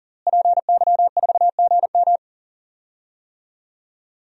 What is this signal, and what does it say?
Morse code, heard as a keyed tone, about 40 wpm, 700 Hz. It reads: PY4GM
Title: DXCC-100 Call Signs - 5 Characters - Repeated 3x - 40wpm